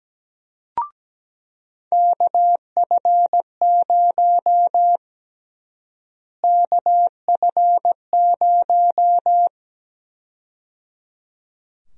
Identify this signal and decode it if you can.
Morse code, heard as a keyed tone, about 17 wpm, 700 Hz. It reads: KF0 KF0
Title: USA Call Sign Prefixes - Repeated 2x - 17wpm